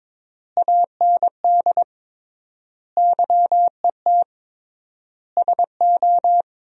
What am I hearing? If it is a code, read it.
Morse code, heard as a keyed tone, about 22 wpm, 700 Hz. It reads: AND YET SO
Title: 3x Word-Spacing - Sentences from Top 500 Words - 22wpm